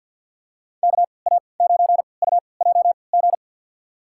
Morse code, heard as a keyed tone, about 52 wpm, spaced as for 28 wpm, 700 Hz. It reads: KA9UJG
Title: USA Call Signs - 6 Characters - 3x Character Spacing - 50wpm